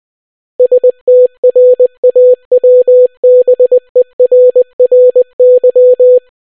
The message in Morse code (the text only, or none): STRAWBERRY